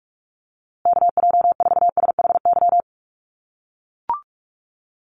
Morse code, heard as a keyed tone, about 44 wpm, 700 Hz. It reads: KJ4SHY
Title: USA Call Signs Repeated 3x 45wpm